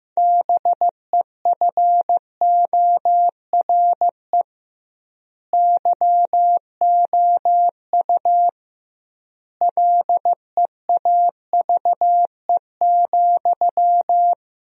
Morse code, heard as a keyed tone, about 15 wpm, 700 Hz. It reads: BEFORE YOU LEAVE,
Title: 2x Word-Spacing - Sentences from Top 400 Words - 15wpm